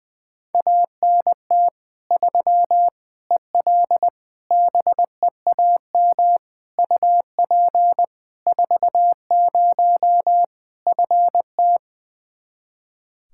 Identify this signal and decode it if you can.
Morse code, heard as a keyed tone, about 20 wpm, 700 Hz. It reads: ANT 3 EL BEAM UP 40 FT
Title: Practice QSO - Exchange 2 - Info - 20wpm